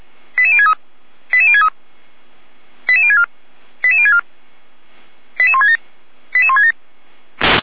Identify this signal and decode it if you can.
Five-tone selective call (ZVEI), heard as quick runs of five tones, tones of about 70 ms. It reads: 88063 88063 88064 88064 88167 88167